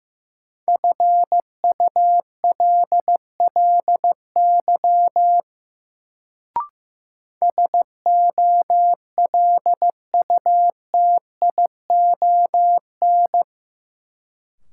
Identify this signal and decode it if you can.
Morse code, heard as a keyed tone, about 15 wpm, 700 Hz. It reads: FULLY SOLUTION
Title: Sentences from Top 1300 Words - New Word Drill Alt - 15wpm